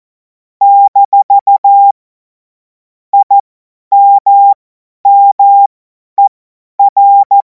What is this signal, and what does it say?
Morse code, heard as a keyed tone, about 14 wpm, spaced as for 10 wpm, 800 Hz. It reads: - IMMER
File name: Morsekurs / Witze / Witz 28. April 2026 50 F70